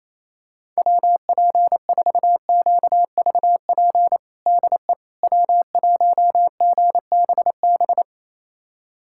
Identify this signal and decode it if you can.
Morse code, heard as a keyed tone, about 28 wpm, 700 Hz. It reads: WP4QVP DE W1GBB
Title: USA Callsign DE Callsign 28wpm